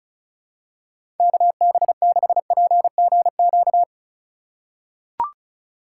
Morse code, heard as a keyed tone, about 34 wpm, 700 Hz. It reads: KB6PGQ
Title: USA Call Signs - 4x Character Spacing - 35wpm